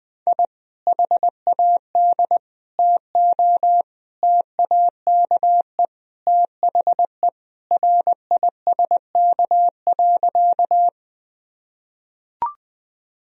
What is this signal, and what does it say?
Morse code, heard as a keyed tone, about 20 wpm, 700 Hz. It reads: I HAD TO TAKE THE RISK.
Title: Sentences from Top 1000 Words - Review Part 4 - 20wpm